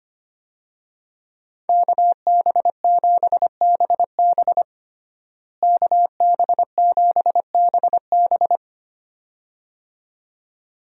Morse code, heard as a keyed tone, about 25 wpm, 700 Hz. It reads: KB7BB KB7BB
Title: USA Call Signs - 5 Characters - Repeated 2x - 25wpm